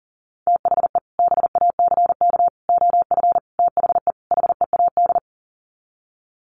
Morse code, heard as a keyed tone, about 40 wpm, 700 Hz. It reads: THE BACK OF THE HEAD
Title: Sets of 5 Words Encore2 40wpm